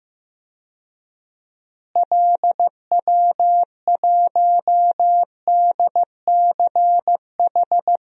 Morse code, heard as a keyed tone, about 15 wpm, 700 Hz. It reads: LW1DCH
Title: DXCC-100 Call Signs - 6 Characters - Repeated 2x - 15wpm